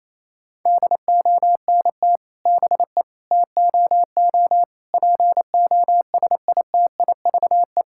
Morse code, heard as a keyed tone, about 28 wpm, 700 Hz. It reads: DONT BE TOO POSITIVE
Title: Sentences from Top 1000 Words - Review Part 5 - 28wpm